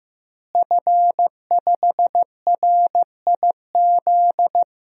Morse code, heard as a keyed tone, about 16 wpm, 700 Hz. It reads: F5RIZ
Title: DXCC-100 Call Signs - 5 Characters - Repeated 3x - 15wpm